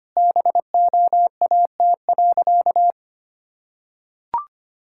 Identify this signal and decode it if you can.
Morse code, heard as a keyed tone, about 25 wpm, 700 Hz. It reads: BOAT.
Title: Sentences from Top 1000 Words - Review Part 6 - 25wpm